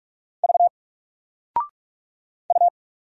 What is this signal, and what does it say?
Morse code, heard as a keyed tone, about 46 wpm, 700 Hz. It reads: V U
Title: Single Letter-Number 45wpm